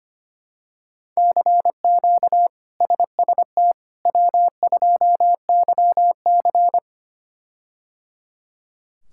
Morse code, heard as a keyed tone, about 25 wpm, 700 Hz. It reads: CQ SST W2YC
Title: K1USN SST QSO Practice - 25wpm